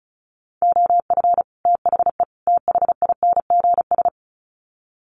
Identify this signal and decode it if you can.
Morse code, heard as a keyed tone, about 34 wpm, 700 Hz. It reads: OF THE THINGS
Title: Sets of 3 Words 35wpm